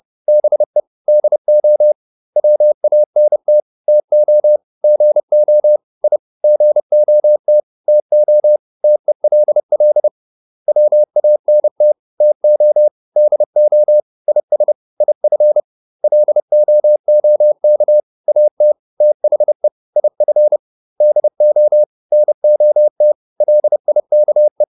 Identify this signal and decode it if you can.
Morse code, heard as a keyed tone, about 30 wpm, 600 Hz. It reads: BE DO WANT TO GO I GOT TO TELL WANT TO DO IS IF LOOK AT THE IF DO NOT LIKE